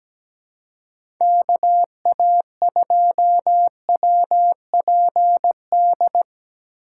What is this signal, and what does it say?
Morse code, heard as a keyed tone, about 17 wpm, 700 Hz. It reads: KA2WPD
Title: USA Call Signs - 2x Character Spacing - 17wpm